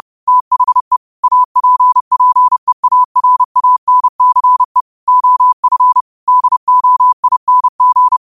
Morse code, heard as a keyed tone, about 30 wpm, 1000 Hz. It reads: THE APPEARANCE OF DOING
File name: qotdfiles / 30wpm